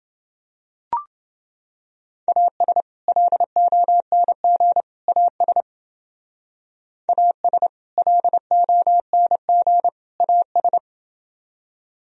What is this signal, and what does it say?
Morse code, heard as a keyed tone, about 28 wpm, 700 Hz. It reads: AS LONG AS AS LONG AS